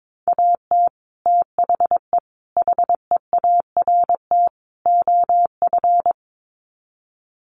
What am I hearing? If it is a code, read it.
Morse code, heard as a keyed tone, about 22 wpm, 700 Hz. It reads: AT THE HEART OF
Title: Sets of 4 Words 22wpm